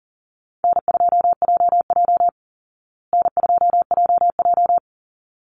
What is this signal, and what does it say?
Morse code, heard as a keyed tone, about 40 wpm, 700 Hz. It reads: N2JJ N2JJ
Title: Call Signs Repeated 3x - 40wpm